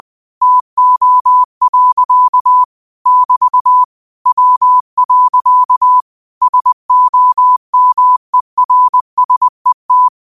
Morse code, heard as a keyed tone, about 20 wpm, 1000 Hz. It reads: TO. <BT> W. SOMERSET